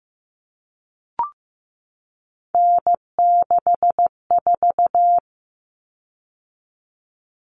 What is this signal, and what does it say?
Morse code, heard as a keyed tone, about 15 wpm, 700 Hz. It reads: N64
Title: Common Cut Numbers - 3-Digit 15wpm